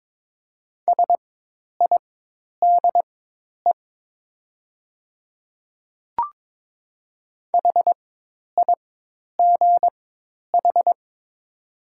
Morse code, heard as a keyed tone, about 22 wpm, spaced as for 10 wpm, 700 Hz. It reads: SIDE HIGH
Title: 4x Character Spacing - Top 200 Words - 22wpm